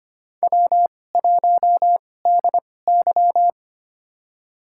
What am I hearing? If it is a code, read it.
Morse code, heard as a keyed tone, about 25 wpm, spaced as for 18 wpm, 700 Hz. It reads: W1DY
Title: USA Call Signs - 2x Character Spacing - 25wpm